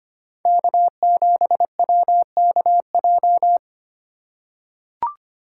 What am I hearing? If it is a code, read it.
Morse code, heard as a keyed tone, about 25 wpm, 700 Hz. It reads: K7WKJ
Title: USA Call Signs - 5 Characters - 3x Character Spacing - 25wpm